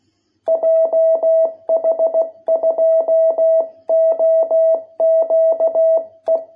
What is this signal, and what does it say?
Morse code, heard as a keyed tone, about 16 wpm, 660 Hz. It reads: JH2OQE